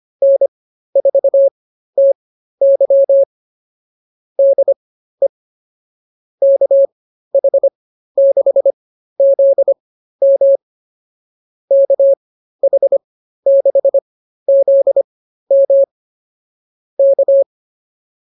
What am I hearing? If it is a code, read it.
Morse code, heard as a keyed tone, about 24 wpm, spaced as for 13 wpm, 550 Hz. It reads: N4TY DE KH6ZM KH6ZM K